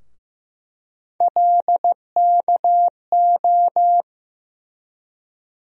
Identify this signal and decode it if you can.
Morse code, heard as a keyed tone, about 15 wpm, 700 Hz. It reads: LKO
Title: USA Call Sign Postfixes - 3 Letters 15wpm